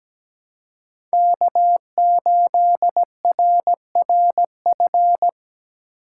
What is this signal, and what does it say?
Morse code, heard as a keyed tone, about 17 wpm, 700 Hz. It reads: K8RRF
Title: USA Call Signs - 3x Character Spacing - 17wpm